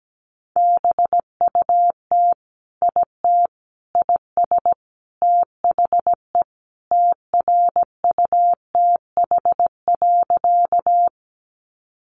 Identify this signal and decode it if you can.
Morse code, heard as a keyed tone, about 17 wpm, 700 Hz. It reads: BUT IT IS THE TRUTH.
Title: Sentences from Top 800 Words 17wpm